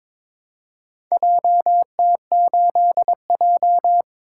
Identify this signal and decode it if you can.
Morse code, heard as a keyed tone, about 22 wpm, 700 Hz. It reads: JT8J
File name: practice / 22wpm / Four Letter-Number Combinations - 22wpm